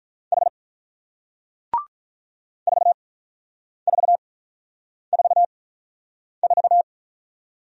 Morse code, heard as a keyed tone, about 43 wpm, spaced as for 9 wpm, 690 Hz. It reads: H 4444
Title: Speed-Racing Single Letter-Number 35wpm